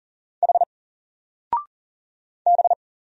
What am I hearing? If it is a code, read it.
Morse code, heard as a keyed tone, about 40 wpm, 700 Hz. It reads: H B
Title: Single Letter-Number 40wpm